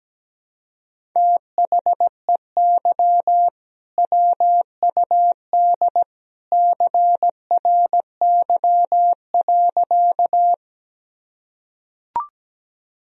Morse code, heard as a keyed tone, about 17 wpm, 700 Hz. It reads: THEY WUD CRY.
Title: Sentences from Top 500 Words and Abbreviations 17wpm